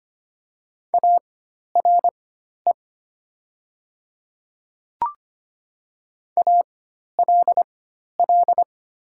Morse code, heard as a keyed tone, about 25 wpm, spaced as for 12 wpm, 700 Hz. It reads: ARE ALL
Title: 4x Character Spacing - Top 100 Words - 25wpm